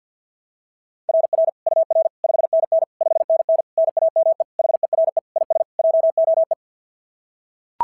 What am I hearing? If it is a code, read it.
Morse code, heard as a keyed tone, about 43 wpm, 660 Hz. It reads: RR UR 5NN 5NN NAME HERE IS JOE
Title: SOTA QSO Practice 50wpm